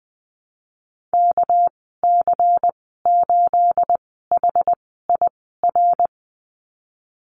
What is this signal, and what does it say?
Morse code, heard as a keyed tone, about 20 wpm, spaced as for 15 wpm, 700 Hz. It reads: KC8HIR
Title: 238 Lesson 38g - Letter q - Call Signs 20wpm